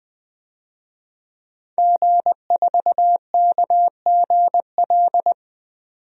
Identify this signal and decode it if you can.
Morse code, heard as a keyed tone, about 20 wpm, 700 Hz. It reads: G4KGL
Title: DXCC-100 Call Signs - 5 Characters - 20wpm